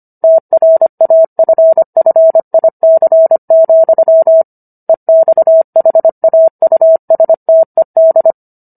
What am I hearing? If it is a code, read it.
Morse code, heard as a keyed tone, about 25 wpm, 650 Hz. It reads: TRAFFIC, EXHAUSTED